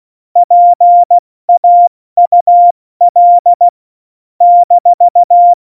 Morse code, heard as a keyed tone, about 16 wpm, 700 Hz. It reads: PAUL -
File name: Morsekurs / Witze / Witz 02. April 2019, 70 F80